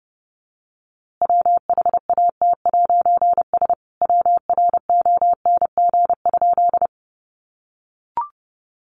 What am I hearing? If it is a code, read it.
Morse code, heard as a keyed tone, about 30 wpm, 700 Hz. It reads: WHAT'S WRONG?
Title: A Sound Of Thunder 30wpm